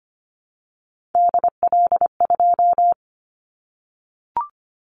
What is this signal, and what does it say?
Morse code, heard as a keyed tone, about 25 wpm, 700 Hz. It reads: DL2
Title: Callsign Prefixes for Top 50 DXCC - 25wpm